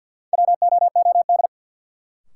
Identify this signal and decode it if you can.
Morse code, heard as a keyed tone, about 48 wpm, 700 Hz. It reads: WOOD
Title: Sentences from Top 1100 Words - New Word Drill Alt - 50wpm